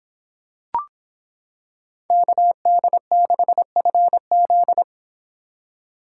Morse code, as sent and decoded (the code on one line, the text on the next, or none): -.- -.. -.... ..-. --..
KD6FZ